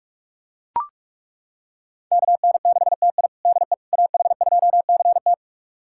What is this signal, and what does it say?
Morse code, heard as a keyed tone, about 41 wpm, 700 Hz. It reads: KN6TI DE AH2CT
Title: USA Callsign DE Callsign - 5 Characters - 45wpm